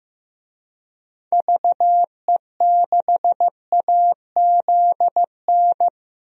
Morse code, heard as a keyed tone, about 15 wpm, 700 Hz. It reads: VE6AZN
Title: DXCC-100 Call Signs - 6 Characters - Repeated 3x - 15wpm